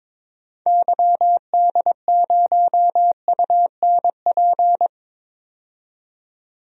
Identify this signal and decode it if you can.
Morse code, heard as a keyed tone, about 22 wpm, 700 Hz. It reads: YD0UNP
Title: DXCC-100 Call Signs - 6 Characters - 22wpm